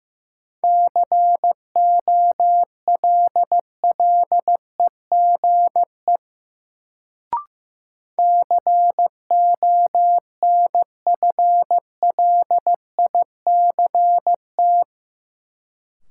Morse code, heard as a keyed tone, about 15 wpm, 700 Hz. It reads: COLLEGE CONFLICT
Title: Sentences from Top 1300 Words - New Word Drill Alt - 15wpm